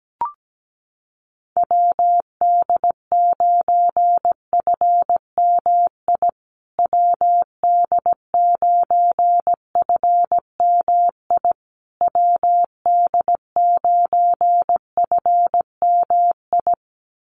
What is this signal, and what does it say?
Morse code, heard as a keyed tone, about 17 wpm, 700 Hz. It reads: WD9FMI WD9FMI WD9FMI
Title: USA Call Signs Repeated 3x 17wpm